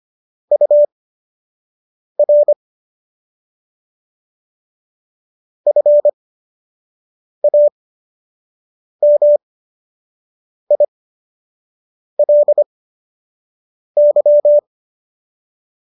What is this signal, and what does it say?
Morse code, heard as a keyed tone, about 25 wpm, spaced as for 6 wpm, 590 Hz. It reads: UR FAMILY